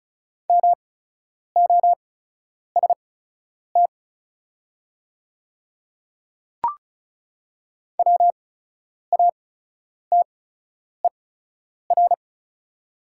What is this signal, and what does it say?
Morse code, heard as a keyed tone, about 36 wpm, spaced as for 10 wpm, 700 Hz. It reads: MOST WATER
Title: 8x Character Spacing - Top 300 Words - 35wpm